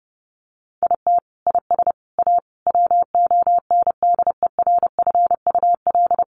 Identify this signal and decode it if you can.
Morse code, heard as a keyed tone, about 30 wpm, 700 Hz. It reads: IT IS A WONDERFUL
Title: Sentences from Top 1300 Words 30wpm